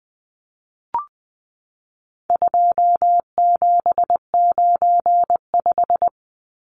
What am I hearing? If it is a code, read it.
Morse code, heard as a keyed tone, about 20 wpm, 700 Hz. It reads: 2795
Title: Numbers - 4-Digit 20wpm